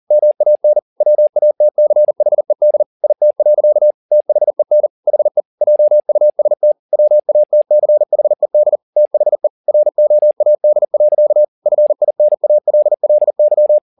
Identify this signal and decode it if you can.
Morse code, heard as a keyed tone, about 40 wpm, 600 Hz. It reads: MAN WATCHED IT. THEN HE JUST WATCHED THE ROAD. FINALLY
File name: the old man and the sea 2 40wpm